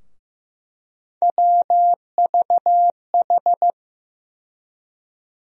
Morse code, heard as a keyed tone, about 15 wpm, 700 Hz. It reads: WVH
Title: USA Call Sign Postfixes - 3 Letters 15wpm